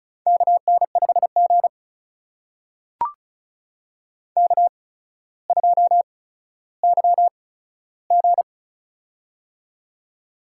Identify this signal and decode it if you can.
Morse code, heard as a keyed tone, about 36 wpm, spaced as for 15 wpm, 700 Hz. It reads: KN5G K2YG
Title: USA Call Signs - 4 Characters - 8x Character Spacing - 35wpm